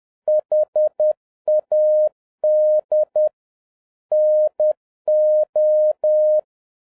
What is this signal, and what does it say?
Morse code, heard as a keyed tone, about 10 wpm, 610 Hz. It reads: HAD NO
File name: 2011-07-mp3 / n36 THE 3.11 2 10wpm